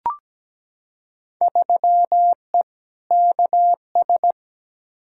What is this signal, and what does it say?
Morse code, heard as a keyed tone, about 17 wpm, 700 Hz. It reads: 3E KS
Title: Field Day Exchanges Encore - 17wpm